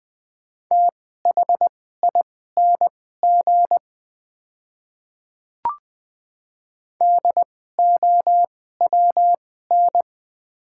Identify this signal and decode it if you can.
Morse code, heard as a keyed tone, about 21 wpm, spaced as for 15 wpm, 700 Hz. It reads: THING DOWN
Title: 2x Character Spacing - Top 100 Words - 20wpm